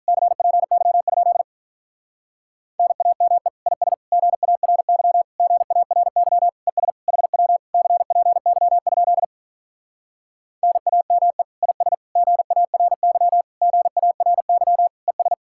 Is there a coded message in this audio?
Yes: Morse code from 0.1 to 15.3 s, heard as a keyed tone, about 46 wpm, 700 Hz. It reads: CPY? NAME IS GARY GARY ES HW CPY? NAME IS GARY GARY ES